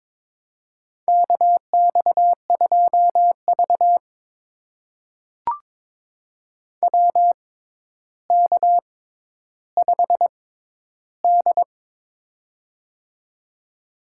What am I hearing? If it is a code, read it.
Morse code, heard as a keyed tone, about 23 wpm, spaced as for 11 wpm, 700 Hz. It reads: KX2V WK5D
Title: USA Call Signs - 4 Characters - 6x Character Spacing - 22wpm